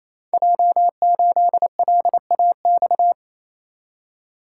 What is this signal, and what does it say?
Morse code, heard as a keyed tone, about 28 wpm, 700 Hz. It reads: J8LAX
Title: Code Groups - 5 Letter:Numbers - 28wpm